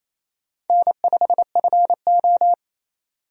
Morse code, heard as a keyed tone, about 28 wpm, 700 Hz. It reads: N5FO
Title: USA Call Signs - 4 Characters - 28wpm